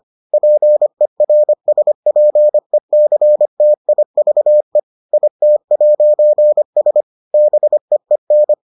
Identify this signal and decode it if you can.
Morse code, heard as a keyed tone, about 25 wpm, 600 Hz. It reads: PERSPECTIVE IT'S BEEN